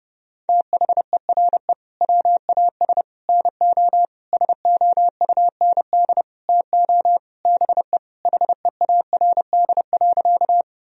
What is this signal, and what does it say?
Morse code, heard as a keyed tone, about 30 wpm, 700 Hz. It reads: THERE WAS NO SOUND TO BE HEARD.
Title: Sentences from Top 1000 Words - Review Part 1 - 30wpm